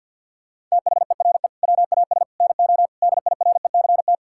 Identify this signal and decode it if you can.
Morse code, heard as a keyed tone, about 50 wpm, 700 Hz. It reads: THERE WAS NO DIRECT
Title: Sentences from Top 1000 Words - Review Part 6 - 50wpm